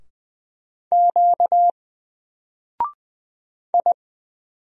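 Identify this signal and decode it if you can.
Morse code, heard as a keyed tone, about 20 wpm, 700 Hz. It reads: Q I
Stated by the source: Single letter and number Morse drill at 20 wpm, from N4LCD.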